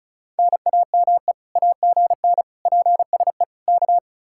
Morse code, heard as a keyed tone, about 44 wpm, 700 Hz. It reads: NAME AGN PSE K